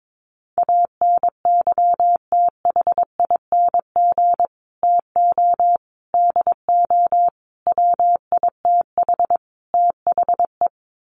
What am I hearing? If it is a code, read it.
Morse code, heard as a keyed tone, about 22 wpm, 700 Hz. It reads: ANYTHING TO DO WITH THE
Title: Sets of 5 Words Encore 22wpm